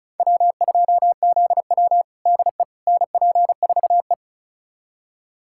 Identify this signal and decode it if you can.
Morse code, heard as a keyed tone, about 36 wpm, 700 Hz. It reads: W2ZW DE NP4E
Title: USA Callsign DE Callsign - 4 Characters - 35wpm